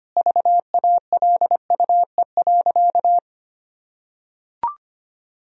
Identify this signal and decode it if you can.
Morse code, heard as a keyed tone, about 25 wpm, 700 Hz. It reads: VALUE.
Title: Sentences from Top 1000 Words - Review Part 4 - 25wpm